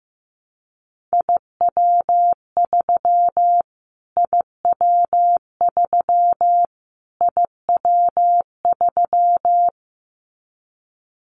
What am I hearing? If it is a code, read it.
Morse code, heard as a keyed tone, about 15 wpm, 700 Hz. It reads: IW3 IW3 IW3
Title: Callsign Prefixes for Top 25 DXCC - Repeated 3x - 15wpm